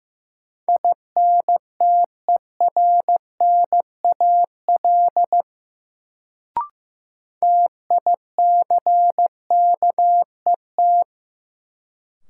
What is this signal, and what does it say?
Morse code, heard as a keyed tone, about 15 wpm, 700 Hz. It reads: INTERNAL TICKET
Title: Sentences from Top 1400 Words - New Word Drill - 15wpm